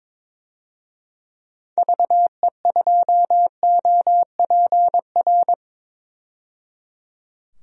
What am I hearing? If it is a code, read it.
Morse code, heard as a keyed tone, about 22 wpm, 700 Hz. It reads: VE2OPR
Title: DXCC-100 Call Signs - 6 Characters - 22wpm